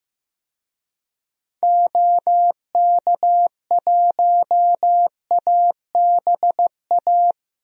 Morse code, heard as a keyed tone, about 15 wpm, 700 Hz. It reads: OK1ABA
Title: DXCC-100 Call Signs - 6 Characters - Repeated 3x - 15wpm